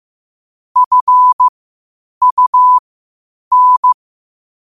Morse code, heard as a keyed tone, about 15 wpm, spaced as for 8 wpm, 1000 Hz. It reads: FUN
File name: qotdfiles / 8wpm